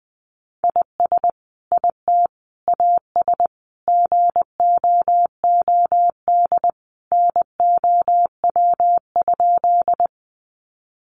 Morse code, heard as a keyed tone, about 20 wpm, 700 Hz. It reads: IS IT AS GOOD NOW?
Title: Sentences from Top 200 Words 20wpm